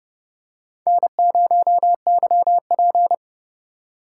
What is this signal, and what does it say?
Morse code, heard as a keyed tone, about 32 wpm, 700 Hz. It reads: N0YP
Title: USA Call Signs - 4x Character Spacing - 30wpm